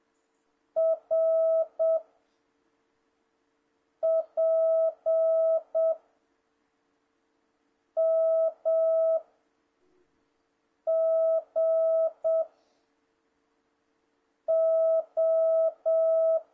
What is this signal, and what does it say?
Morse code, heard as a keyed tone, about 7 wpm, 650 Hz. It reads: R P M G O
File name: jh8cbh / 2026cw